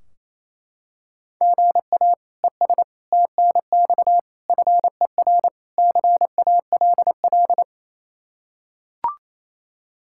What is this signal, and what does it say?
Morse code, heard as a keyed tone, about 28 wpm, 700 Hz. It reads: GA ES TNX FER CALL
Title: K1USN SST QSO Practice 28wpm